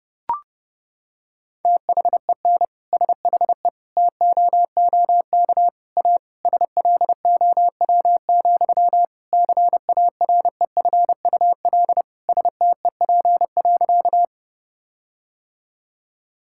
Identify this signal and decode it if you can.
Morse code, heard as a keyed tone, about 32 wpm, 700 Hz. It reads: THEN SHE TOOK A SLOW, CAREFUL STEP.
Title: Sentences from Top 1600 Words 30wpm